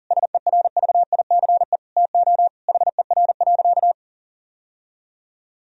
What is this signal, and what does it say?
Morse code, heard as a keyed tone, about 40 wpm, 700 Hz. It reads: SERVICE TO HER.
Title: Sentences from Top 1000 Words - Review Part 4 - 40wpm